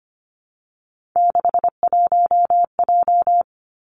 Morse code, heard as a keyed tone, about 25 wpm, 700 Hz. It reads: 61J
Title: Three Challenging Letter-Number Combinations 25wpm